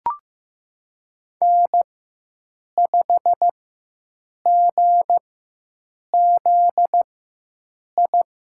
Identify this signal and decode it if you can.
Morse code, heard as a keyed tone, about 15 wpm, spaced as for 7 wpm, 700 Hz. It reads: N5GZI
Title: USA Call Signs - 4x Character Spacing - 15wpm